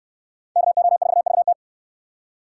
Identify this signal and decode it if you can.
Morse code, heard as a keyed tone, about 68 wpm, 700 Hz. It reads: KO4VT